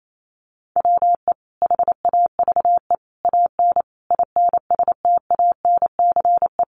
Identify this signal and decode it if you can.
Morse code, heard as a keyed tone, about 28 wpm, 700 Hz. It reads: WE HAVE AN INSTANCE
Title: Sentences from Top 800 Words 28wpm